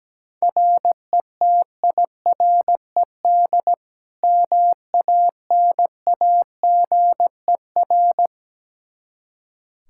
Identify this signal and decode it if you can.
Morse code, heard as a keyed tone, about 17 wpm, 700 Hz. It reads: RETIRED MANAGER
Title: K1USN SST QSO Practice 17wpm